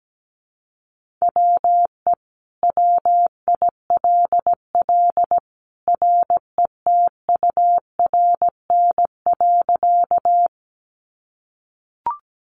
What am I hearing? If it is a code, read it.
Morse code, heard as a keyed tone, about 17 wpm, 700 Hz. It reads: WE WILL RETURN.
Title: Sentences from Top 800 Words 17wpm